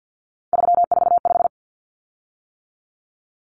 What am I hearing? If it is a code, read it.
Morse code, heard as a keyed tone, about 50 wpm, 700 Hz. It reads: 345
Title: Common Cut Numbers - 3-Digit 50wpm